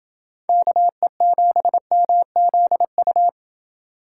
Morse code, heard as a keyed tone, about 28 wpm, 700 Hz. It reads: KE7MZU